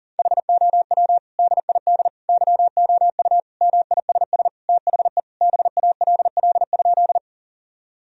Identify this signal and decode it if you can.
Morse code, heard as a keyed tone, about 40 wpm, 700 Hz. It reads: HOW DID YOU MISS THE BALL?
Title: Sentences from Top 1000 Words - Review Part 3 - 40wpm